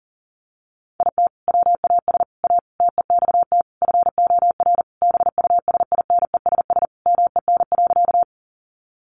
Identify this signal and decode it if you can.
Morse code, heard as a keyed tone, about 40 wpm, 700 Hz. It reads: IT WAS A TEXT FOR BUSINESS MEN.
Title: Sentences from Top 1200 Words 40wpm